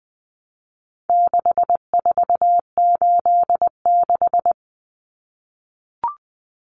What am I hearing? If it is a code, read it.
Morse code, heard as a keyed tone, about 20 wpm, 700 Hz. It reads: 6486
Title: Numbers - 4-Digit 20wpm